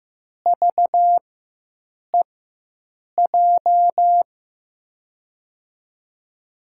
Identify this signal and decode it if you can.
Morse code, heard as a keyed tone, about 15 wpm, spaced as for 7 wpm, 700 Hz. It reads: VEJ
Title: USA Call Signs - 4x Character Spacing - 15wpm